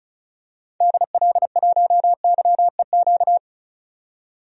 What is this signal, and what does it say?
Morse code, heard as a keyed tone, about 34 wpm, 700 Hz. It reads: DL1YEQ
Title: DXCC-100 Call Signs - 6 Characters - 35wpm